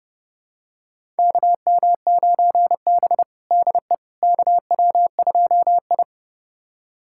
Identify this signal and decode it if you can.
Morse code, heard as a keyed tone, about 32 wpm, 700 Hz. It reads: KM9B DE KW2I
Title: USA Callsign DE Callsign - 4 Characters - 30wpm